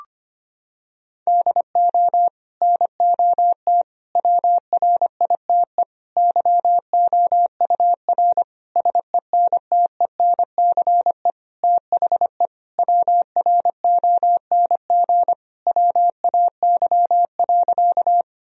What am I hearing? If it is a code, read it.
Morse code, heard as a keyed tone, about 25 wpm, 700 Hz. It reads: DO NOT WRITE YOUR SENTENCE THE WRONG WAY.
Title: Sentences from Top 1000 Words - Review Part 1 - 25wpm